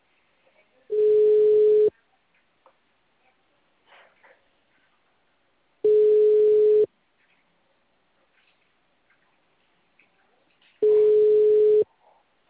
Ringback tone (Europe, 425 Hz).